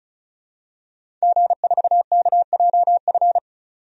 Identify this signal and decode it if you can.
Morse code, heard as a keyed tone, about 34 wpm, 700 Hz. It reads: G4KJF
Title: DXCC-100 Call Signs - 5 Characters - 35wpm